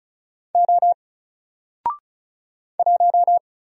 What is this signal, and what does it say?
Morse code, heard as a keyed tone, about 36 wpm, 700 Hz. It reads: O 1
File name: Single-Letter-Number-35wpm